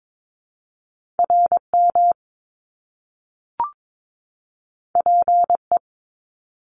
Morse code, heard as a keyed tone, about 22 wpm, 700 Hz. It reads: RM PE
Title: Top Two Letter Combinations in Words 22wpm